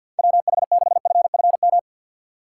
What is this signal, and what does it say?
Morse code, heard as a keyed tone, about 53 wpm, 700 Hz. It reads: WH6FFM
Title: USA Call Signs - 6 Characters - 3x Character Spacing - 50wpm